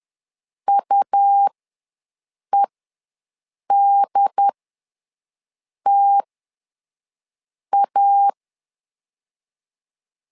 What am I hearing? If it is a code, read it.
Morse code, heard as a keyed tone, about 10 wpm, spaced as for 5 wpm, 800 Hz. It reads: UEDTA
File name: LEZIONI / 2 UHDN / 2 UHD ws 10wpm 2058